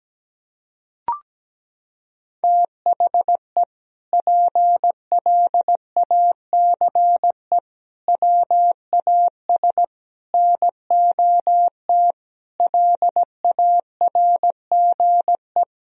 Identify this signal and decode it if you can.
Morse code, heard as a keyed tone, about 17 wpm, 700 Hz. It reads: THE PLACE WAS NOT LARGE
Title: Sentences from Top 1000 Words - Review Part 1 - 17wpm